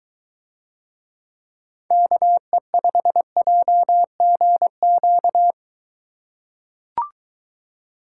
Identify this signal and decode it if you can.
Morse code, heard as a keyed tone, about 22 wpm, 700 Hz. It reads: KE5JGQ